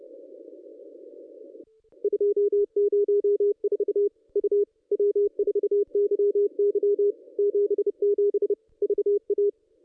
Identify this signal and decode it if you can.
Morse code, heard as a keyed tone, about 30 wpm, 410 Hz. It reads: 204 U W4YY 77 VA
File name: contests / ss-cw-k5tr-op-n5rz -1200